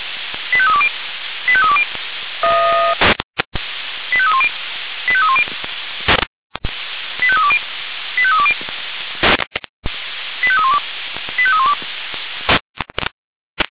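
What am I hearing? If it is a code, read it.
Five-tone selective call (ZVEI), heard as quick runs of five tones, tones of about 70 ms. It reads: 85320 85320 85310 85310 85320 85320 85312 85312
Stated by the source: ZVEI five-tone selective call 85320 received by SDR.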